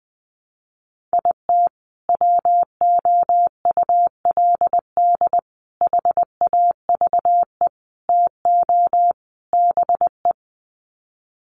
Morse code, heard as a keyed tone, about 20 wpm, 700 Hz. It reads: IT WOULD HAVE TO BE
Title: Sets of 5 Words Encore 20wpm